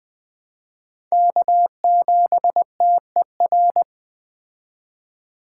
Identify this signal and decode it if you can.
Morse code, heard as a keyed tone, about 21 wpm, 700 Hz. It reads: K7TER
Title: Speed Racing - USA Call Signs 17wpm